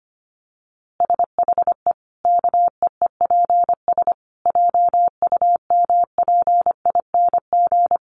Morse code, heard as a keyed tone, about 25 wpm, 700 Hz. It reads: SHE KEEPS JUMPING
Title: Easy Phrases 25wpm